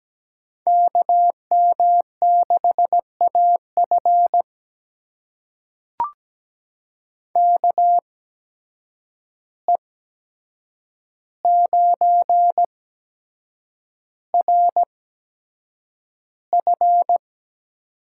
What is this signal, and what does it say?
Morse code, heard as a keyed tone, about 17 wpm, spaced as for 7 wpm, 700 Hz. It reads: KM6AF KE9RF
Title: USA Call Signs - 5 Characters - 8x Character Spacing - 17wpm